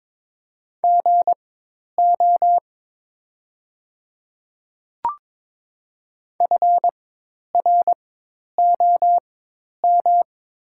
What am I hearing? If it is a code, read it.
Morse code, heard as a keyed tone, about 22 wpm, spaced as for 10 wpm, 700 Hz. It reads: GO FROM
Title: 4x Character Spacing - Top 200 Words - 22wpm